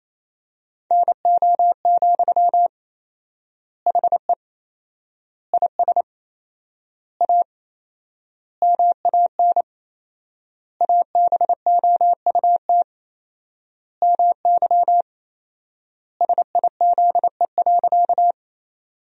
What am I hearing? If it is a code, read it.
Morse code, heard as a keyed tone, about 27 wpm, 700 Hz. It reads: NO, HE IS A MAN ABOUT MY SIZE.
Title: 4x Word-Spacing - Sentences from Top 400 Words - 28wpm